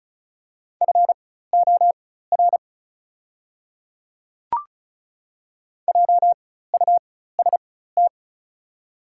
Morse code, heard as a keyed tone, about 34 wpm, spaced as for 16 wpm, 700 Hz. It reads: FOR JUST
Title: 4x Character Spacing - Top 200 Words - 35wpm